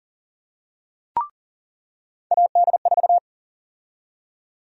Morse code, heard as a keyed tone, about 40 wpm, 700 Hz. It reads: AD4
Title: USA Call Sign Prefixes - 40wpm